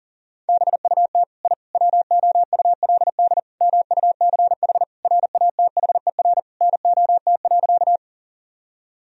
Morse code, heard as a keyed tone, about 40 wpm, 700 Hz. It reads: BUT I WOULD MUCH RATHER NOT.
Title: Sentences from Top 1000 Words - Review Part 4 - 40wpm